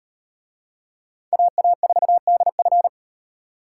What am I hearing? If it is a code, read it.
Morse code, heard as a keyed tone, about 36 wpm, 700 Hz. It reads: AA4DF